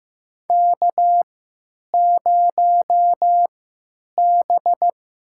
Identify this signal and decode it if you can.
Morse code, heard as a keyed tone, about 16 wpm, spaced as for 9 wpm, 700 Hz. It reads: K0B
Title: USA Call Signs - 3x Character Spacing - 15wpm